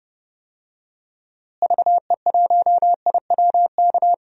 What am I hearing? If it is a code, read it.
Morse code, heard as a keyed tone, about 29 wpm, 700 Hz. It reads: VE1IWK
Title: DXCC-100 Call Signs - 6 Characters - Repeated 2x - 30wpm